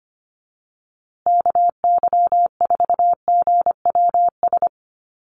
Morse code, heard as a keyed tone, about 25 wpm, 700 Hz. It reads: KY4GWS
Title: USA Call Signs Repeated 3x 25wpm